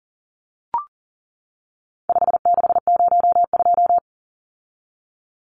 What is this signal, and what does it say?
Morse code, heard as a keyed tone, about 38 wpm, 700 Hz. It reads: 5602
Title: Numbers - 4-Digit 40wpm